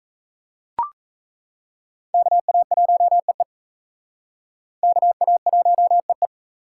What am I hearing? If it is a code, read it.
Morse code, heard as a keyed tone, about 34 wpm, 700 Hz. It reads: KA1EE KA1EE
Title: Speed Racing - USA 2x2 Call Signs 28wpm